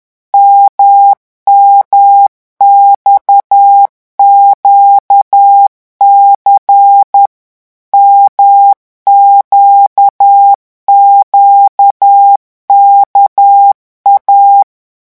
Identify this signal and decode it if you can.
Morse code, heard as a keyed tone, about 10 wpm, 800 Hz. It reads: MMXQC MQQKA
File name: morse mp3 / revision2-vitesse 10 mots